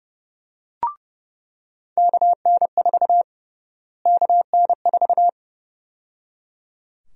Morse code, heard as a keyed tone, about 32 wpm, 700 Hz. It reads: KN4 KN4
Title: USA Call Sign Prefixes - Repeated 2x - 30wpm